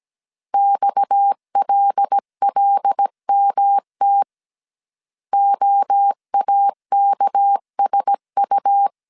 Morse code, heard as a keyed tone, about 17 wpm, 800 Hz. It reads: XLLMT OAKSU
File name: CW / DET / 1 6 1500C S A 16wpm 2635